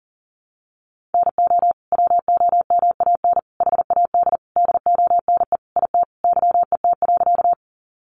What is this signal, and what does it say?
Morse code, heard as a keyed tone, about 40 wpm, 700 Hz. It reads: NO WOMAN HAD DONE IT YET.
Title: Sentences from Top 1100 Words 40wpm